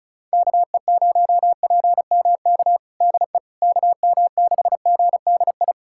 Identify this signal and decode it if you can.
Morse code, heard as a keyed tone, about 36 wpm, 700 Hz. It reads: KE0PMK DE KM6GDI
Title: USA Callsign DE Callsign - 6 Characters - 35wpm